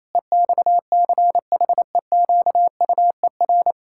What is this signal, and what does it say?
Morse code, heard as a keyed tone, about 28 wpm, 700 Hz. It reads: EXCHEQUER
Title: Pangram Challenge v2 - 28wpm